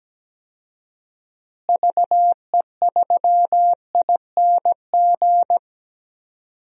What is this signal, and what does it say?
Morse code, heard as a keyed tone, about 17 wpm, 700 Hz. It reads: VE3ING
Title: DXCC-100 Call Signs - 6 Characters - Repeated 3x - 17wpm